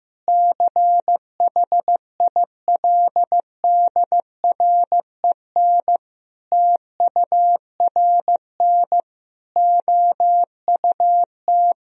Morse code, heard as a keyed tone, about 15 wpm, 700 Hz. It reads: CHILDREN TURN OUT
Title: Sentences from Top 1000 Words - Review Part 3 - 15wpm